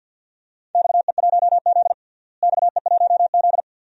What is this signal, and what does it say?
Morse code, heard as a keyed tone, about 52 wpm, 700 Hz. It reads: XE1Z XE1Z
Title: DXCC-100 Call Signs - 4 Characters - Repeated 2x - 50wpm